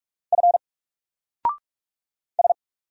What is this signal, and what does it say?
Morse code, heard as a keyed tone, about 44 wpm, 700 Hz. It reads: F S
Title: Single Letter-Number Alt 45wpm